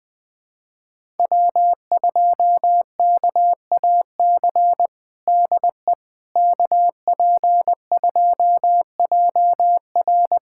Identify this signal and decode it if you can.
Morse code, heard as a keyed tone, about 20 wpm, 700 Hz. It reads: W2KAC DE KP2JR
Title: USA Callsign DE Callsign - 5 Characters - 20wpm